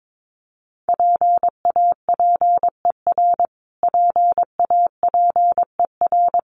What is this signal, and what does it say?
Morse code, heard as a keyed tone, about 22 wpm, 700 Hz. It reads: PAPER PAPER
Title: Top 500 Words Repeated 22wpm